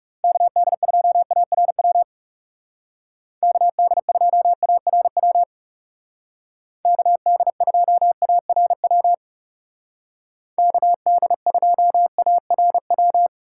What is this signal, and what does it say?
Morse code, heard as a keyed tone, about 35 wpm, 700 Hz. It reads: KD2ARW KD2ARW KD2ARW KD2ARW
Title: Speed Racing - USA 2x3 Call Signs 30wpm